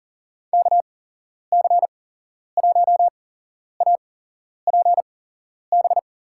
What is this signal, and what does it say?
Morse code, heard as a keyed tone, about 40 wpm, spaced as for 11 wpm, 700 Hz. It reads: KC1APB
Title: USA Call Signs - 6 Characters - 8x Character Spacing - 40wpm